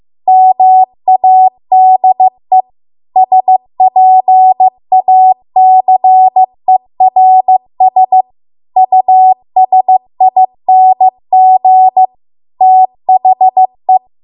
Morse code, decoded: MADE SPACERS USING THE

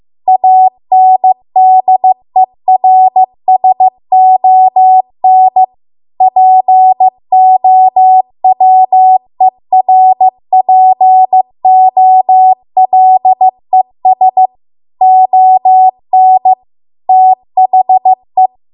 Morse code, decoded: ANDERSON POWERPOLES ON THE

